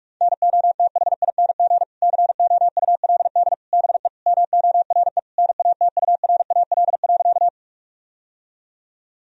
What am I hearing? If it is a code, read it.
Morse code, heard as a keyed tone, about 46 wpm, 700 Hz. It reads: NOTHING COULD BE MORE NATURAL.
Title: Sentences from Top 1000 Words - Review Part 4 - 45wpm